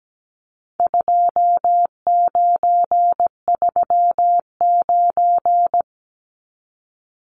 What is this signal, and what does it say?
Morse code, heard as a keyed tone, about 17 wpm, 700 Hz. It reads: 2939
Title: Numbers - 4-Digit 17wpm